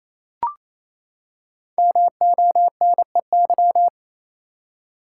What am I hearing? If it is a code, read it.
Morse code, heard as a keyed tone, about 28 wpm, 700 Hz. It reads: MONEY
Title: Five Letter Words - 28wpm